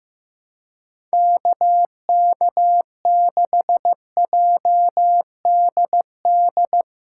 Morse code, heard as a keyed tone, about 15 wpm, 700 Hz. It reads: KK6JDD
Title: USA Call Signs - 3x Character Spacing - 15wpm